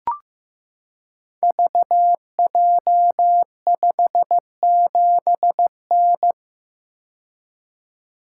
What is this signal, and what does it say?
Morse code, heard as a keyed tone, about 15 wpm, 700 Hz. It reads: VJ57N
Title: Code Groups - 5 Letter:Numbers - 15wpm